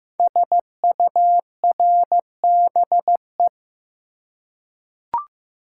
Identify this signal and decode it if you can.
Morse code, heard as a keyed tone, about 15 wpm, 700 Hz. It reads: SURBE
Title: Code Groups - 5 Letters - 15wpm